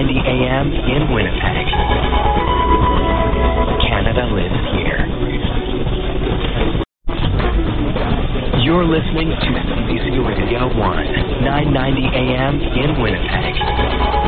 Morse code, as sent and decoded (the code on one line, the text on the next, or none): none